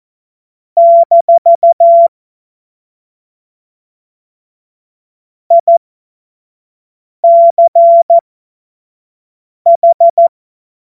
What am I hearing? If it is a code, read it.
Morse code, heard as a keyed tone, about 14 wpm, spaced as for 5 wpm, 680 Hz. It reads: - ICH
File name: Morsekurs / Witze / Witz 23. August 2022 25 F70